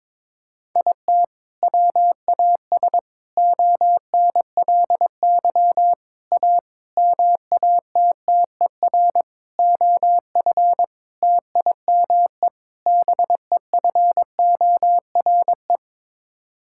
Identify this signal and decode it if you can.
Morse code, heard as a keyed tone, about 22 wpm, 700 Hz. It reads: IT WAS ONLY A MATTER OF TIME BEFORE
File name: practice / 22wpm / Sets of 8 Words - 22wpm